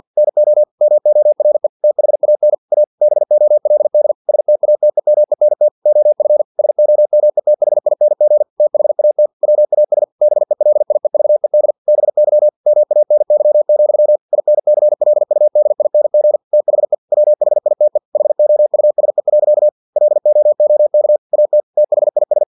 Morse code, heard as a keyed tone, about 50 wpm, 600 Hz. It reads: NO MORE THAN A BOLD STATEMENT OF SOMETHING THAT WAS BELIEVED BY MANY, INCLUDING THE WHITE HOUSE. LOOK AT THIS
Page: Never 49wpm